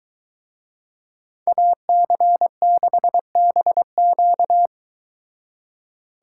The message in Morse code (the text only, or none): AC6BQ